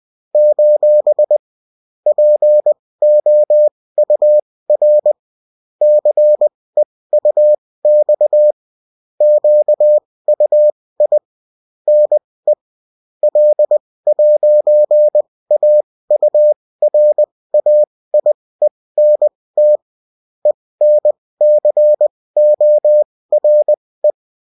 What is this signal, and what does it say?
Morse code, heard as a keyed tone, about 20 wpm, spaced as for 16 wpm, 600 Hz. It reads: : POUR CEUX QUI NE L'AURAIENT ENCORE